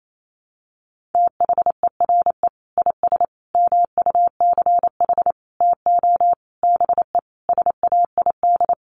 Morse code, heard as a keyed tone, about 28 wpm, 700 Hz. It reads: THERE IS MUCH TO BE SAID
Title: Sentences from Top 1300 Words 28wpm